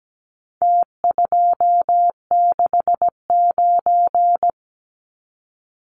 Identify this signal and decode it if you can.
Morse code, heard as a keyed tone, about 17 wpm, 700 Hz. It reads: T269
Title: Numbers - Contest Style - 4-Digit 17wpm